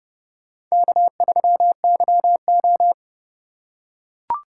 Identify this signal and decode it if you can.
Morse code, heard as a keyed tone, about 29 wpm, 700 Hz. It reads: K3YO